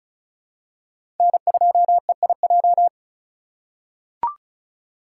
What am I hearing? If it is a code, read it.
Morse code, heard as a keyed tone, about 34 wpm, 700 Hz. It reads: N2EIJ
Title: USA Call Signs - 5 Characters - 2x Character Spacing - 35wpm